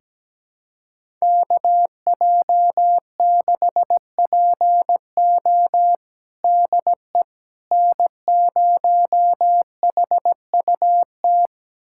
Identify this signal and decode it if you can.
Morse code, heard as a keyed tone, about 17 wpm, 700 Hz. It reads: KJ6PO DE N0HUT
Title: USA Callsign DE Callsign - 5 Characters - 17wpm